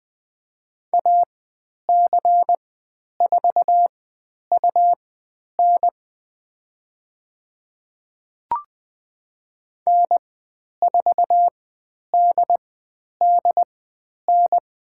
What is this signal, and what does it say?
Morse code, heard as a keyed tone, about 20 wpm, spaced as for 10 wpm, 700 Hz. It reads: AC4UN N4DDN
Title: 102 Lesson 17g - Letter u - Call Signs 20wpm